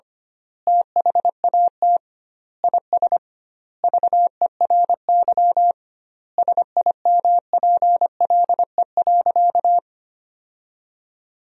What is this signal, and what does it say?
Morse code, heard as a keyed tone, about 25 wpm, 700 Hz. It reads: THAT IS VERY SIMPLE.
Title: 2x Word-Spacing - Sentences from Top 500 Words - 25wpm